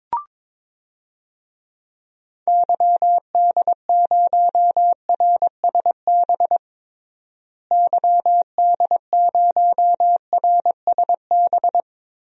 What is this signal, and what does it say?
Morse code, heard as a keyed tone, about 21 wpm, 700 Hz. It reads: YD0RSB YD0RSB
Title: DXCC-100 Call Signs - 6 Characters - Repeated 3x - 22wpm